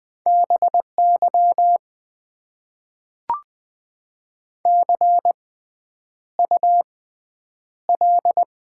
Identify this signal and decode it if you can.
Morse code, heard as a keyed tone, about 20 wpm, spaced as for 9 wpm, 700 Hz. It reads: BY CUL
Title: Top 90 QSO Elements - 6x Character Spacing - 20wpm